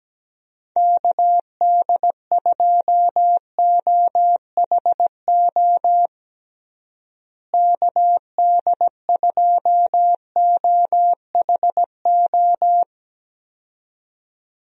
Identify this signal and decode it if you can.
Morse code, heard as a keyed tone, about 17 wpm, 700 Hz. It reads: KD2OHO KD2OHO
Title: USA Call Signs - 6 Characters - Repeated 2x - 17wpm